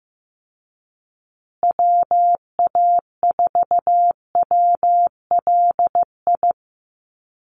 Morse code, heard as a keyed tone, about 15 wpm, 700 Hz. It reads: WA4WLI
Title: Call Signs Repeated 2x - 15wpm